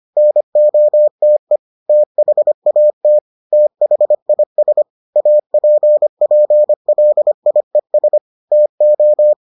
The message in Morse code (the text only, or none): NOTE THAT THIS APPLIES TO